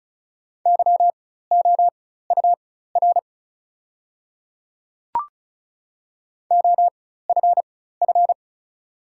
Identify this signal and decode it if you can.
Morse code, heard as a keyed tone, about 36 wpm, spaced as for 16 wpm, 700 Hz. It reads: YOUR OFF
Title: 4x Character Spacing - Top 200 Words - 35wpm